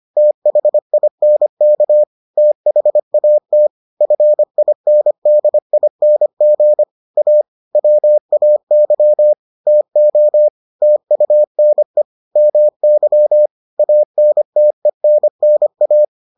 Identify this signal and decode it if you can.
Morse code, heard as a keyed tone, about 25 wpm, 600 Hz. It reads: THINK THAT FINDING A WAY TO TUNE MY ANTENNA